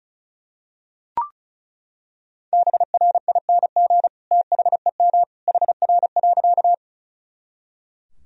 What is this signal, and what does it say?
Morse code, heard as a keyed tone, about 36 wpm, 700 Hz. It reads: BRING THEM HR.
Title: Sentences from Top 500 Words and Abbreviations 35wpm